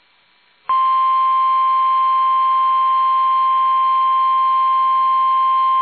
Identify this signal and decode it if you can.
Emergency Alert System (SAME) tones (the 1050 Hz attention tone). It is heard from 0.8 to 4.9 s.
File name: eas / KUSO / KUSO